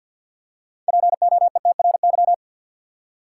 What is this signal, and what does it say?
Morse code, heard as a keyed tone, about 52 wpm, 700 Hz. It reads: POETRY
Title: Six Letter Words - 50wpm